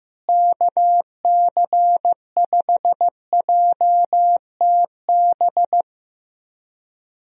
Morse code, heard as a keyed tone, about 15 wpm, 700 Hz. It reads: KC5JTB
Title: USA Call Signs - 6 Characters - Repeated 3x - 15wpm